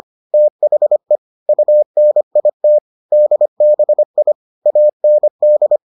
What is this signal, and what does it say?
Morse code, heard as a keyed tone, about 25 wpm, 600 Hz. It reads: THE UNIT DBI AND